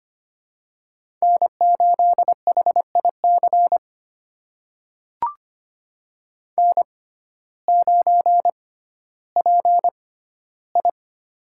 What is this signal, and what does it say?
Morse code, heard as a keyed tone, about 25 wpm, spaced as for 14 wpm, 700 Hz. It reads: N8HIC N9PI
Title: USA Call Signs - 6x Character Spacing - 25wpm